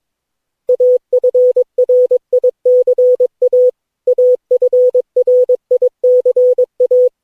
Morse code, heard as a keyed tone, about 22 wpm, 500 Hz. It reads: AFRICA AFRICA
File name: cw / Session9 cw training